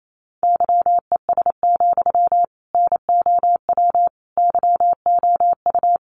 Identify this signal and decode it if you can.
Morse code, heard as a keyed tone, about 28 wpm, 700 Hz. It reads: YES, NOW YOU
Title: Sentences from Top 800 Words 28wpm